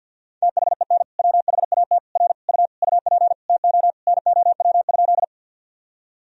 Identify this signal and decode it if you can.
Morse code, heard as a keyed tone, about 49 wpm, 700 Hz. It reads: THEN WHAT R U UP TO NOW?
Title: Sentences from Top 100 Words and Abbreviations 50wpm